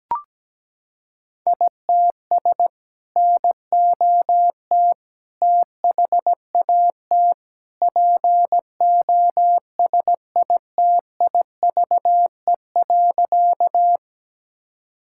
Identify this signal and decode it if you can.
Morse code, heard as a keyed tone, about 17 wpm, 700 Hz. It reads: ITS NOT THAT POSITIVE.
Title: Sentences from Top 1000 Words - Review Part 5 - 17wpm